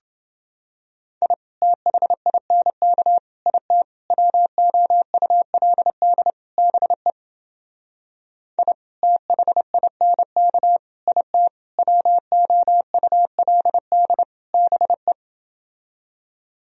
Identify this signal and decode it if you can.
Morse code, heard as a keyed tone, about 29 wpm, 700 Hz. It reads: I THINK IT WOULD BE I THINK IT WOULD BE